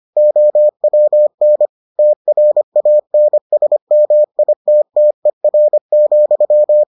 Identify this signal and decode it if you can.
Morse code, heard as a keyed tone, about 25 wpm, 600 Hz. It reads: OWN TRANSMITTER,